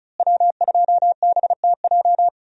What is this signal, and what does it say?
Morse code, heard as a keyed tone, about 34 wpm, 700 Hz. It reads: W2BTJ
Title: USA Call Signs - 5 Characters - Repeated 2x - 35wpm